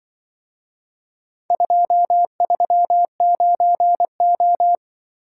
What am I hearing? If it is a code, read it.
Morse code, heard as a keyed tone, about 24 wpm, 700 Hz. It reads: 239O
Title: Speed-Racing - Four Letter-Number Combinations - 20wpm